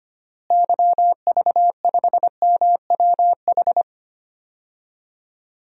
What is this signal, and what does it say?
Morse code, heard as a keyed tone, about 25 wpm, 700 Hz. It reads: YV5MWH
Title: DXCC-100 Call Signs - 6 Characters - Repeated 2x - 25wpm